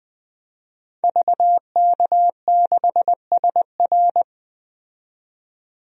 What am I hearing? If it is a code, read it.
Morse code, heard as a keyed tone, about 20 wpm, 700 Hz. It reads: VK6SR
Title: DXCC-100 Call Signs - 5 Characters - 20wpm